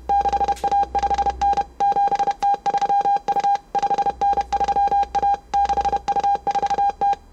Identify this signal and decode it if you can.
Morse code, heard as a keyed tone, about 31 wpm, 800 Hz. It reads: 6A5N 7T3U 5N3A 6U4T